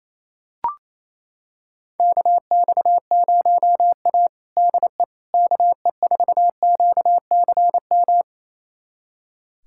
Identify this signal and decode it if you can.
Morse code, heard as a keyed tone, about 28 wpm, 700 Hz. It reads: KX0A DE KE4QCM
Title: USA Callsign DE Callsign 28wpm